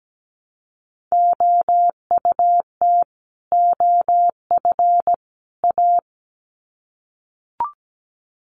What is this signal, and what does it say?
Morse code, heard as a keyed tone, about 17 wpm, 700 Hz. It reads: OUT OF A
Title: Sets of 3 Words 17wpm